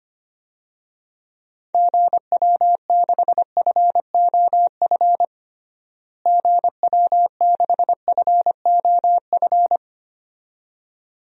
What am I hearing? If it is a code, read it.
Morse code, heard as a keyed tone, about 25 wpm, 700 Hz. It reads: GW6FOF GW6FOF
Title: DXCC-100 Call Signs - 6 Characters - Repeated 2x - 25wpm